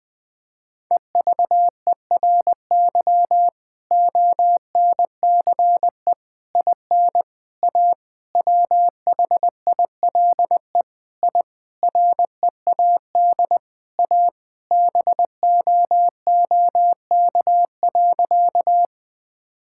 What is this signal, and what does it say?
Morse code, heard as a keyed tone, about 20 wpm, 700 Hz. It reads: EVERY ONCE IN A WHILE I READ A BOOK.
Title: Sentences with 5-Word Phrases - 20wpm